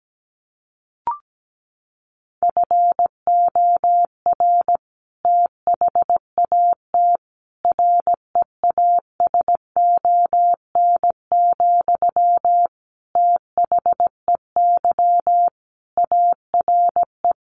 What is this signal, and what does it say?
Morse code, heard as a keyed tone, about 17 wpm, 700 Hz. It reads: FOR THAT REASON, THEY ARE
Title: Sentences from Top 1500 Words 17wpm